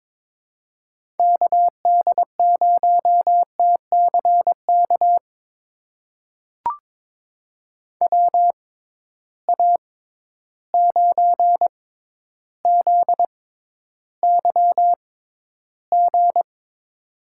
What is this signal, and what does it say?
Morse code, heard as a keyed tone, about 23 wpm, spaced as for 11 wpm, 700 Hz. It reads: KD0TCK WA9ZYG
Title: USA Call Signs - 6 Characters - 6x Character Spacing - 22wpm